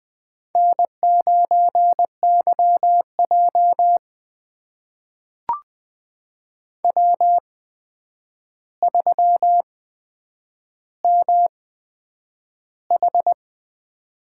Morse code, heard as a keyed tone, about 20 wpm, spaced as for 9 wpm, 700 Hz. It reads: N9YJ W3MH
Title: USA Call Signs - 4 Characters - 8x Character Spacing - 20wpm